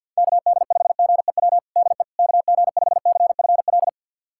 Morse code, heard as a keyed tone, about 51 wpm, 700 Hz. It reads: KD5GEW DE KG5CFL
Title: USA Callsign DE Callsign - 6 Characters - 50wpm